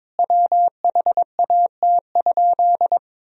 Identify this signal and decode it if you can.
Morse code, heard as a keyed tone, about 22 wpm, 700 Hz. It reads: WHAT?
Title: Sentences from Top 100 Words and Abbreviations 22wpm